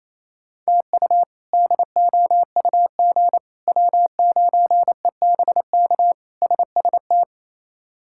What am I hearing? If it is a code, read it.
Morse code, heard as a keyed tone, about 27 wpm, 700 Hz. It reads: TU DOUG W9EBK SST
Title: K1USN SST QSO Practice - 28wpm